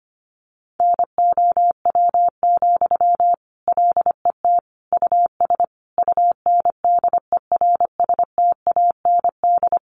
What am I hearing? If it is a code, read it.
Morse code, heard as a keyed tone, about 25 wpm, 700 Hz. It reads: NOW, LET US UNDERSTAND
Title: Sentences from Top 1500 Words 25wpm